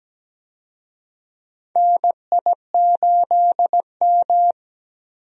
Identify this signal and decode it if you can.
Morse code, heard as a keyed tone, about 17 wpm, 700 Hz. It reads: NI8M
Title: USA Call Signs - 4 Characters - 17wpm